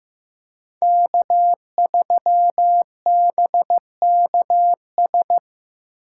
Morse code, heard as a keyed tone, about 18 wpm, 700 Hz. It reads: K3BKS
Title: Speed Racing - USA 1x3 Call Signs 15wpm